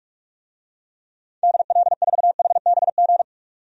Morse code, heard as a keyed tone, about 47 wpm, 700 Hz. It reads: DL4HBG
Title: DXCC-100 Call Signs - 6 Characters - 45wpm